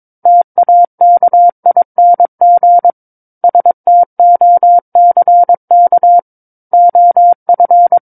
Morse code, heard as a keyed tone, about 22 wpm, 700 Hz. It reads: TAKING STOCK OF
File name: Book One / 13-Chapter 13 - 22 WPM